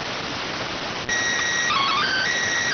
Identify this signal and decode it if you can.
SSTV (slow-scan television), PD 290.